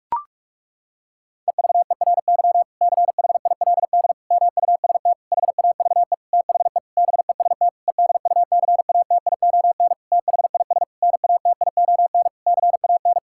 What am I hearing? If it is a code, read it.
Morse code, heard as a keyed tone, about 46 wpm, 700 Hz. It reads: EVERY CHILD MUST HAVE THE BEST EDUCATION THIS NATION CAN
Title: Sentences from Top 1000 Words - Review Part 6 - 45wpm